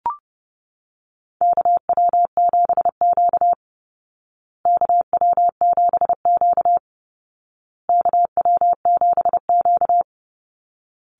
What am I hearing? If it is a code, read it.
Morse code, heard as a keyed tone, about 30 wpm, 700 Hz. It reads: KW7Q KW7Q KW7Q
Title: Call Signs Repeated 2x - 30wpm